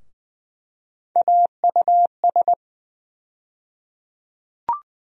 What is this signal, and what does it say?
Morse code, heard as a keyed tone, about 20 wpm, 700 Hz. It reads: AUS
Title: USA Call Sign Postfixes - 3 Letters 20wpm